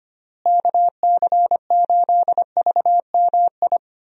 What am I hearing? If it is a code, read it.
Morse code, heard as a keyed tone, about 25 wpm, 700 Hz. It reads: KC8VMI
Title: USA Call Signs - 6 Characters - Repeated 3x - 25wpm